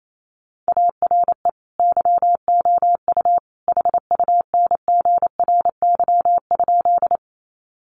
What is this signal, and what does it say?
Morse code, heard as a keyed tone, about 28 wpm, 700 Hz. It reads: ARE YOU HUNGRY?
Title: Sentences from Top 1400 Words 28wpm